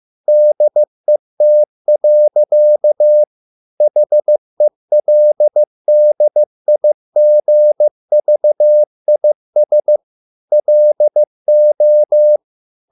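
Morse code, heard as a keyed tone, about 15 wpm, 600 Hz. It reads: DET. HELDIGVIS LO
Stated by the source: Morse code at a slow 15 wpm from CW med Gnister.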